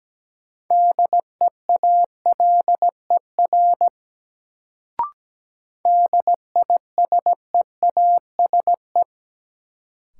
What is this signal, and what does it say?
Morse code, heard as a keyed tone, about 17 wpm, 700 Hz. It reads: DEALER DISEASE
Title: Sentences from Top 1100 Words - New Word Drill Alt - 17wpm